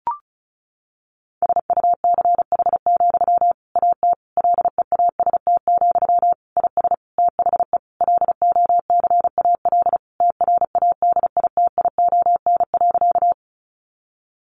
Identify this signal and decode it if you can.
Morse code, heard as a keyed tone, about 34 wpm, 700 Hz. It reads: SUCH, AT LEAST, IS THE LOCAL TRADITION.
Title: Sentences from Top 1500 Words 35wpm